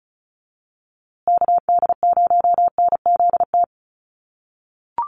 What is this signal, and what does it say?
Morse code, heard as a keyed tone, about 34 wpm, 700 Hz. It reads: KD0NZT
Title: USA Call Signs Repeated 3x 35wpm